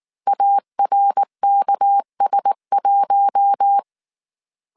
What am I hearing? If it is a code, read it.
Morse code, heard as a keyed tone, about 19 wpm, 800 Hz. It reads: ARKS1